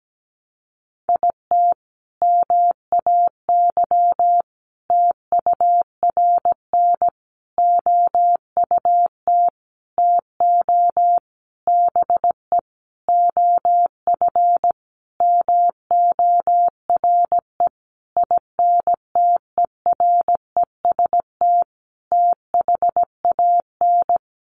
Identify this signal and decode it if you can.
Morse code, heard as a keyed tone, about 17 wpm, 700 Hz. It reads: IT MAY TURN OUT TO BE OF MORE INTEREST THAN
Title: Sentences from Top 400 Words 17wpm